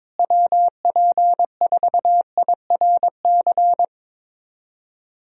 Morse code, heard as a keyed tone, about 22 wpm, 700 Hz. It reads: WP4IRC
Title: USA Call Signs - 6 Characters - Repeated 3x - 22wpm